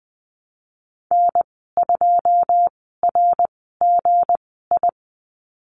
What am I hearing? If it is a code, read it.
Morse code, heard as a keyed tone, about 20 wpm, spaced as for 14 wpm, 700 Hz. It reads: N2RGI
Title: 161 Lesson 26g - Letter g - Call Signs 20wpm